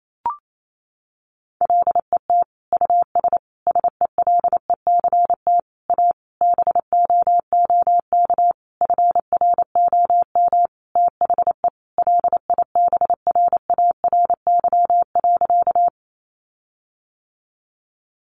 Morse code, heard as a keyed tone, about 28 wpm, 700 Hz. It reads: LET US SELECT A BOOK FROM THE LIBRARY.
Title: Sentences from Top 1300 Words 28wpm